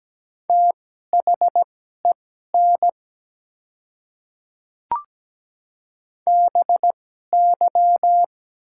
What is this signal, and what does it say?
Morse code, heard as a keyed tone, about 17 wpm, spaced as for 12 wpm, 700 Hz. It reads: THEN BY
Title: 2x Character Spacing - Top 100 Words - 17wpm